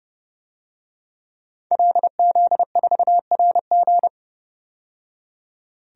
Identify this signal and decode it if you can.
Morse code, heard as a keyed tone, about 30 wpm, 700 Hz. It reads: LZ4RG
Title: DXCC-100 Call Signs - 5 Characters - 30wpm